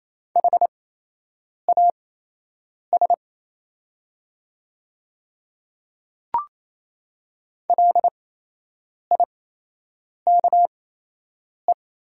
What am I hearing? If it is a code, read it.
Morse code, heard as a keyed tone, about 28 wpm, spaced as for 8 wpm, 700 Hz. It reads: HAS LIKE
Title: 8x Character Spacing - Top 200 Words - 28wpm